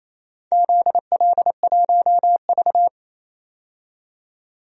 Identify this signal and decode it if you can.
Morse code, heard as a keyed tone, about 29 wpm, 690 Hz. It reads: ZL1V
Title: DXCC-100 Call Signs - 4 Characters - 28wpm